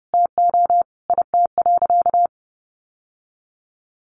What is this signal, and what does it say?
Morse code, heard as a keyed tone, about 30 wpm, 700 Hz. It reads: TO IT.
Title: Sentences from Top 1200 Words 30wpm